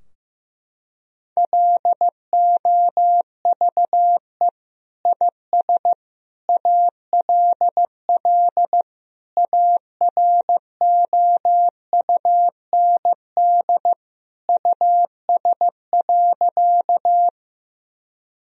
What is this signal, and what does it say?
Morse code, heard as a keyed tone, about 15 wpm, 700 Hz. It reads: LOVE IS ALL AROUND US.
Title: Sentences from Top 1000 Words - Review Part 3 - 15wpm